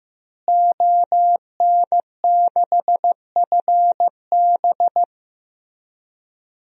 Morse code, heard as a keyed tone, about 15 wpm, 700 Hz. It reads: ON6FB
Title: DXCC-100 Call Signs - 5 Characters - Repeated 3x - 15wpm